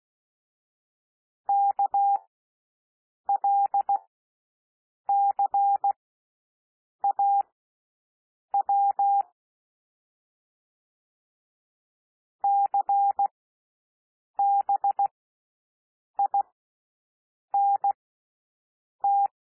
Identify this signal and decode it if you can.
Morse code, heard as a keyed tone, about 16 wpm, spaced as for 6 wpm, 800 Hz. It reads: KLCAW CBINT
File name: cwmp3 / L1 SOUND LETTERS 16 wpm